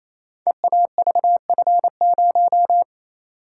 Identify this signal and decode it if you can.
Morse code, heard as a keyed tone, about 29 wpm, 700 Hz. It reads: EAVF0
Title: Code Groups - 5 Letter:Numbers - 28wpm